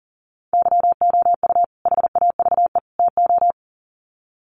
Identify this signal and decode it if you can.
Morse code, heard as a keyed tone, about 40 wpm, 700 Hz. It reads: YOU HAVE TO
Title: Sets of 3 Words 40wpm